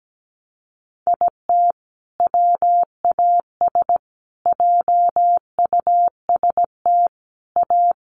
Morse code, heard as a keyed tone, about 17 wpm, 700 Hz. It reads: IT WAS JUST A
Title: Sentences from Top 1200 Words 17wpm